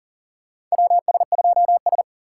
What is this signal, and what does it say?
Morse code, heard as a keyed tone, about 40 wpm, 700 Hz. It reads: WS2S
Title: USA Call Signs - 4 Characters - 2x Character Spacing - 40wpm